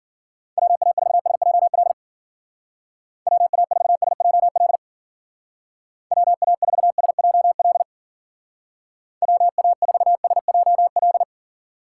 Morse code, heard as a keyed tone, about 50 wpm, 700 Hz. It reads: WA4SJL WA4SJL WA4SJL WA4SJL
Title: Speed Racing - USA 2x3 Call Signs 40wpm